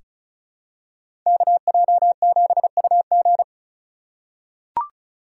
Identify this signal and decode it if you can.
Morse code, heard as a keyed tone, about 36 wpm, 700 Hz. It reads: KJ7UG
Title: USA Call Signs - 3x Character Spacing - 35wpm